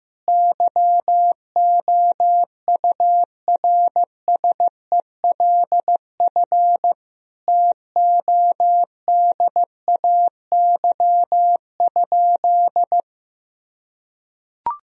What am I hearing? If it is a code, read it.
Morse code, heard as a keyed tone, about 15 wpm, 700 Hz. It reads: YOURSELF TODAY?
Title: Sentences from Top 1000 Words - Review Part 4 - 15wpm